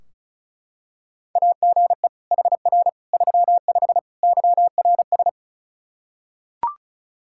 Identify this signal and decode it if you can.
Morse code, heard as a keyed tone, about 36 wpm, 700 Hz. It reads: AGE HR 35 YRS
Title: K1USN SST QSO Practice 35wpm